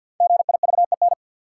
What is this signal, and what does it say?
Morse code, heard as a keyed tone, about 53 wpm, 700 Hz. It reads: GIVEN
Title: Sentences from Top 1400 Words - New Word Drill Alt - 50wpm